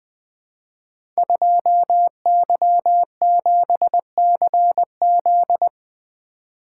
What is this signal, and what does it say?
Morse code, heard as a keyed tone, about 20 wpm, 700 Hz. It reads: 2Y7CZ
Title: Code Groups - 5 Letter:Numbers - 20wpm